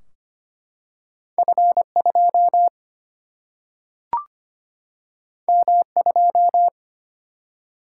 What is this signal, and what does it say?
Morse code, heard as a keyed tone, about 25 wpm, 700 Hz. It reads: F2 M2
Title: Single-Letter Single-Number 25wpm